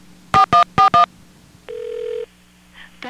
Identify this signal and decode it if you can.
DTMF, heard as telephone touch tones, tones of about 120 ms. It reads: 4141